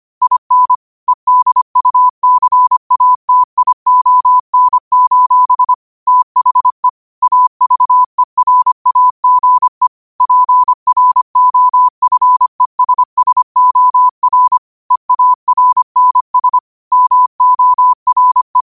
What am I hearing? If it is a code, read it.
Morse code, heard as a keyed tone, about 25 wpm, 1000 Hz. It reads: IN EDUCATION: THE AVERAGE PROFESSOR EARNS MORE
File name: qotdfiles / 25wpm